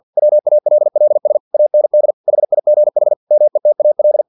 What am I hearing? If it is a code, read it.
Morse code, heard as a keyed tone, about 50 wpm, 600 Hz. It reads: WALLS AND HIGH METAL
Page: Never 49wpm